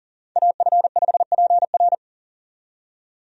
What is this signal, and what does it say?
Morse code, heard as a keyed tone, about 40 wpm, 700 Hz. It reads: AF5PR
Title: USA Call Signs - 5 Characters - 40wpm